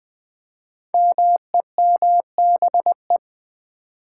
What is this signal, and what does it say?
Morse code, heard as a keyed tone, about 20 wpm, 700 Hz. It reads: MEMBE
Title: Build A Word - 5 Letters At A Time - 10-Letter Words - 20wpm